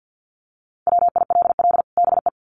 Morse code, heard as a keyed tone, about 50 wpm, 700 Hz. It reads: WILL BE
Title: Sets of 2 Words 50wpm